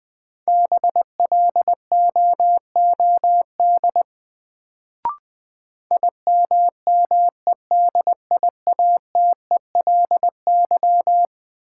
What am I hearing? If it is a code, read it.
Morse code, heard as a keyed tone, about 20 wpm, 700 Hz. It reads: BLOOD IMMEDIATELY
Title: Sentences from Top 1100 Words - New Word Drill Alt - 20wpm